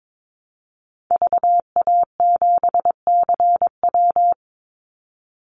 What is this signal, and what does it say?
Morse code, heard as a keyed tone, about 22 wpm, 700 Hz. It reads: VA7CW
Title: Call Signs - 22wpm